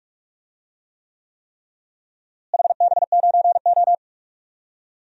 Morse code, heard as a keyed tone, about 46 wpm, 700 Hz. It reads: HB9O
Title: DXCC-100 Call Signs - 4 Characters - Repeated 3x - 45wpm